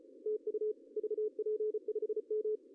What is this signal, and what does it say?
Morse code, heard as a keyed tone, about 32 wpm, 400 Hz, with radio noise. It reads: TU VP5M
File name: CW / cqww-cw-k5tr